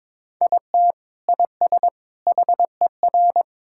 Morse code, heard as a keyed tone, about 21 wpm, 700 Hz. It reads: IT IS HER
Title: Sentences from Top 1600 Words 22wpm